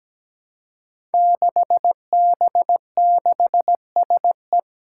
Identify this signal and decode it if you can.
Morse code, heard as a keyed tone, about 17 wpm, 700 Hz. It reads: 6B6SE
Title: Five Challenging Letter-Number Combinations 17wpm